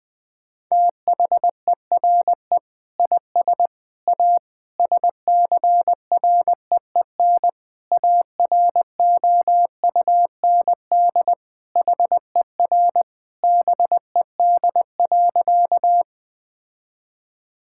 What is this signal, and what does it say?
Morse code, heard as a keyed tone, about 20 wpm, 700 Hz. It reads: THERE IS A SCREEN AROUND HER BED.
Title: Sentences from Top 1000 Words - Review Part 6 - 20wpm